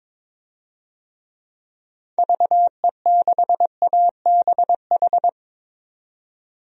Morse code, heard as a keyed tone, about 22 wpm, 700 Hz. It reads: VE6ABH
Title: DXCC-100 Call Signs - 6 Characters - 22wpm